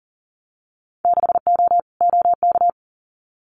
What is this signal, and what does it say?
Morse code, heard as a keyed tone, about 40 wpm, 700 Hz. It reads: 6O OK
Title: Winter Field Day Exchanges Encore 40wpm